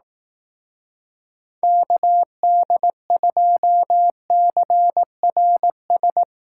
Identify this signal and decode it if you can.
Morse code, heard as a keyed tone, about 18 wpm, 700 Hz. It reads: KD2CRS
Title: Speed Racing - USA 2x3 Call Signs 15wpm